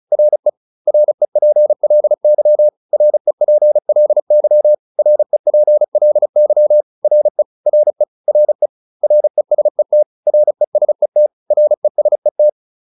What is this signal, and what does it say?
Morse code, heard as a keyed tone, about 34 wpm, 600 Hz. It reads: RE REPLY REPLY REPLY RE RE RE RESET RESET RESET